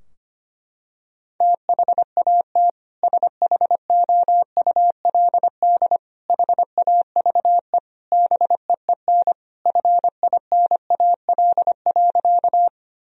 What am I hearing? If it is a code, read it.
Morse code, heard as a keyed tone, about 25 wpm, 700 Hz. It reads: THAT SHOULD HAVE BEEN FINAL.
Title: Sentences from Top 1000 Words - Review Part 2 - 25wpm